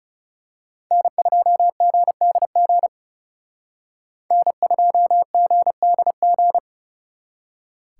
Morse code, heard as a keyed tone, about 34 wpm, 700 Hz. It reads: N2GDG N2GDG
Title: Speed Racing - USA 1x3 Call Signs 30wpm